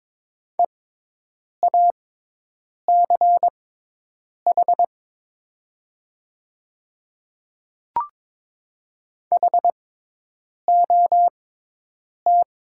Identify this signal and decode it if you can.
Morse code, heard as a keyed tone, about 22 wpm, spaced as for 8 wpm, 700 Hz. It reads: EACH HOT
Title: 6x Character Spacing - Top 100 Words - 22wpm